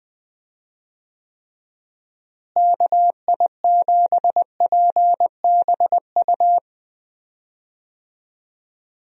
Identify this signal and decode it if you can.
Morse code, heard as a keyed tone, about 20 wpm, 700 Hz. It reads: KI7PBU
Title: USA Call Signs - 6 Characters - Repeated 3x - 20wpm